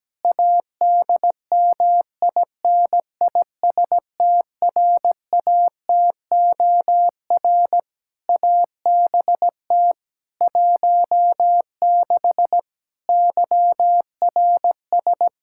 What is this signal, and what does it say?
Morse code, heard as a keyed tone, about 17 wpm, 700 Hz. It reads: ADMINISTRATOR ABT 16 YRS
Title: Practice QSO - Exchange 3 - More Info - 17wpm